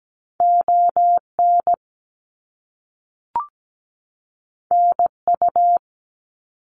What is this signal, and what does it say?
Morse code, heard as a keyed tone, about 17 wpm, 700 Hz. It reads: ON NU
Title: Canadian Provinces and Territories 17wpm